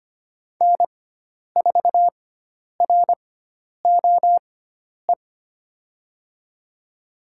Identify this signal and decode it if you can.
Morse code, heard as a keyed tone, about 25 wpm, spaced as for 10 wpm, 700 Hz. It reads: N4ROE
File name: Lesson 16g - Letter c - Call Signs 25wpm